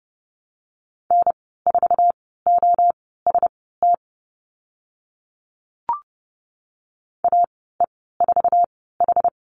Morse code, heard as a keyed tone, about 30 wpm, spaced as for 17 wpm, 700 Hz. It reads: N4OST AE4H
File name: Lesson 15g - Number 5 - Call Signs 30wpm